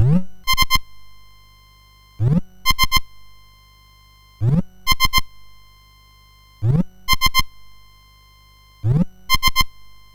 There is no code.